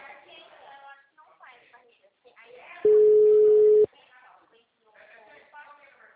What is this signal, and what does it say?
Ringback tone (Europe, 425 Hz), with speech.
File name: out- -2017- - - .45557